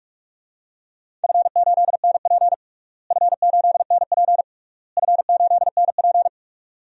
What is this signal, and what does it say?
Morse code, heard as a keyed tone, about 46 wpm, 690 Hz. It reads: F8NP F8NP F8NP
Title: DXCC-100 Call Signs - 4 Characters - Repeated 3x - 45wpm